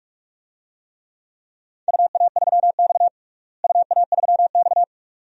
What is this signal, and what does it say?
Morse code, heard as a keyed tone, about 41 wpm, 700 Hz. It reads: UA3X UA3X
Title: DXCC-100 Call Signs - 4 Characters - Repeated 2x - 45wpm